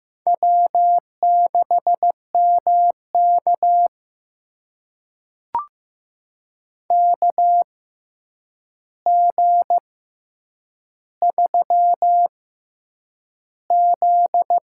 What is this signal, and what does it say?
Morse code, heard as a keyed tone, about 15 wpm, spaced as for 8 wpm, 700 Hz. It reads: W6MK KG3Z
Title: USA Call Signs - 4 Characters - 6x Character Spacing - 15wpm